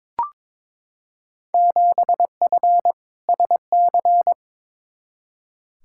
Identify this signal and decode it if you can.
Morse code, heard as a keyed tone, about 22 wpm, 700 Hz. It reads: 7F SC
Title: Field Day Exchanges Encore - 22wpm